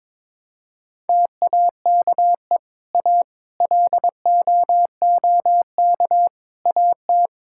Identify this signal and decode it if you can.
Morse code, heard as a keyed tone, about 22 wpm, 700 Hz. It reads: TAKE A LOOK AT
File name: practice / 22wpm / Sentences with 4-Word Phrases - 22wpm